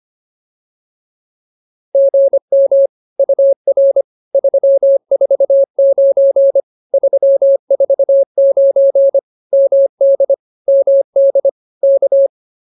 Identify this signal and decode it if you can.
Morse code, heard as a keyed tone, about 25 wpm, 590 Hz. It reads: GM UR 349 349 MD MD K